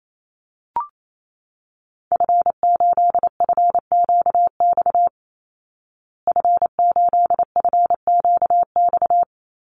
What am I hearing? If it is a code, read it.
Morse code, heard as a keyed tone, about 28 wpm, 700 Hz. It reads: F8FQX F8FQX
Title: Call Signs Repeated 3x - 28wpm